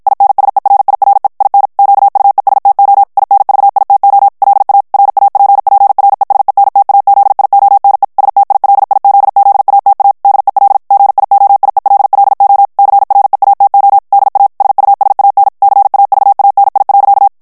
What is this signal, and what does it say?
Morse code, heard as a keyed tone, about 53 wpm, 800 Hz. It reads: INSERIRE IN QUESTO INVITO LA RAPPRESENTAZIONE STILIZZATA DEL GIOIELLO CREATO DA SUSAN CAVANI.
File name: ITALIEN texte clair 1 vitesse 50 mots minute